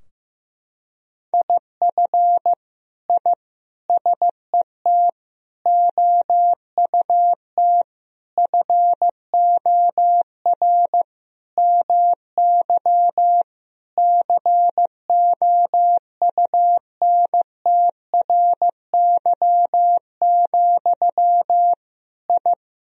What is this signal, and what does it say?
Morse code, heard as a keyed tone, about 15 wpm, 700 Hz. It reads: IF I SET OUT FOR MY COUNTRY, I
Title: Sentences from Top 1000 Words - Review Part 2 - 15wpm